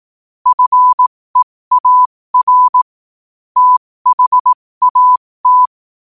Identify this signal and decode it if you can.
Morse code, heard as a keyed tone, about 18 wpm, spaced as for 15 wpm, 1000 Hz. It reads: FEAR THAT
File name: qotdfiles / 15wpm